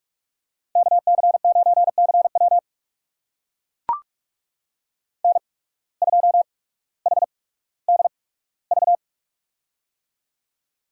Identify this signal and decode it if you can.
Morse code, heard as a keyed tone, about 44 wpm, spaced as for 19 wpm, 700 Hz. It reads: KC9CW N2HDV
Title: USA Call Signs - 5 Characters - 8x Character Spacing - 45wpm